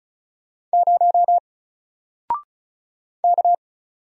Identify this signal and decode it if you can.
Morse code, heard as a keyed tone, about 34 wpm, 700 Hz. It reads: 0 K
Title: Single Letter-Number Alt 35wpm